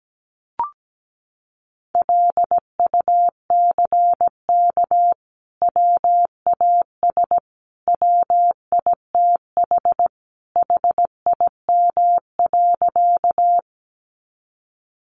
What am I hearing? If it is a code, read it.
Morse code, heard as a keyed tone, about 17 wpm, 700 Hz. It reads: LUCK WAS WITH HIM.
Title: Sentences from Top 1500 Words 17wpm